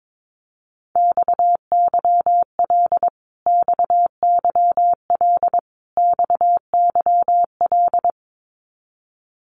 Morse code, heard as a keyed tone, about 22 wpm, 700 Hz. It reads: XYL XYL XYL
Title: Abbreviations Repeated 3x - 22wpm